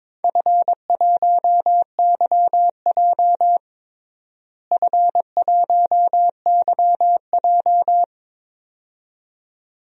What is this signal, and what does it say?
Morse code, heard as a keyed tone, about 22 wpm, 700 Hz. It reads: F1YJ F1YJ
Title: DXCC-100 Call Signs - 4 Characters - Repeated 2x - 22wpm